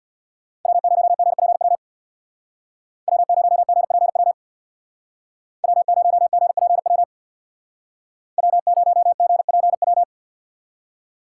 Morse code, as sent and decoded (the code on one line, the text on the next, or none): .-- ----- --. .--. .-- / .-- ----- --. .--. .-- / .-- ----- --. .--. .-- / .-- ----- --. .--. .--
W0GPW W0GPW W0GPW W0GPW